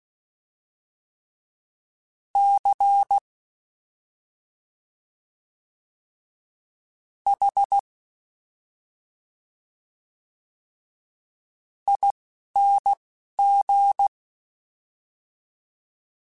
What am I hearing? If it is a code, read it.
Morse code, heard as a keyed tone, about 16 wpm, spaced as for 4 wpm, 800 Hz. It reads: CHING
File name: Special INGs-mp3s / Special INGs-mp3s / Slow ing @10